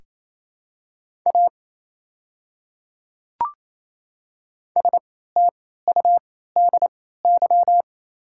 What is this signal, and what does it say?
Morse code, heard as a keyed tone, about 27 wpm, spaced as for 16 wpm, 700 Hz. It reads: A STUDY
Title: 3x Character Spacing - Top 300 Words - 28wpm